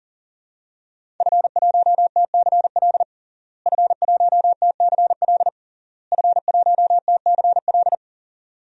Morse code, heard as a keyed tone, about 37 wpm, 700 Hz. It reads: F1TCL F1TCL F1TCL
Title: DXCC-100 Call Signs - 5 Characters - Repeated 3x - 40wpm